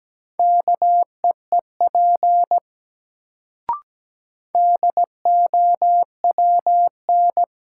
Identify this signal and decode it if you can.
Morse code, heard as a keyed tone, about 17 wpm, 700 Hz. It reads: KEEP DOWN
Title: Sentences from Top 300 Words - New Word Drill - 17wpm